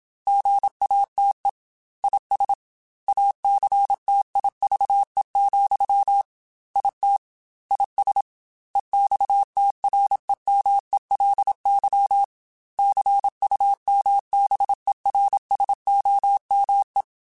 Morse code, heard as a keyed tone, about 27 wpm, 800 Hz. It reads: GATE IS ACTIVE, IT IS EXTREMELY CUMBERSOME
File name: Technical Subjects / Technical Subjects-mp3s / AND Gate @25